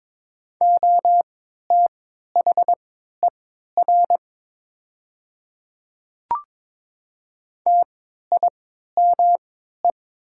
Morse code, heard as a keyed tone, about 22 wpm, spaced as for 12 wpm, 690 Hz. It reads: OTHER TIME
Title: 3x Character Spacing - Top 200 Words - 22wpm